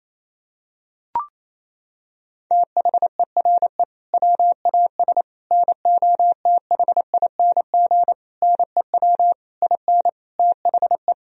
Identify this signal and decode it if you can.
Morse code, heard as a keyed tone, about 28 wpm, 700 Hz. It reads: THERE WAS NOTHING NEW IN THE
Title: Sentences from Top 1000 Words - Review Part 3 - 28wpm